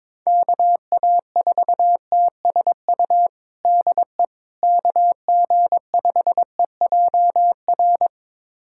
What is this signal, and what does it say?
Morse code, heard as a keyed tone, about 22 wpm, 700 Hz. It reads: KA4TSU DE KG5EJR
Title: USA Callsign DE Callsign - 6 Characters - 22wpm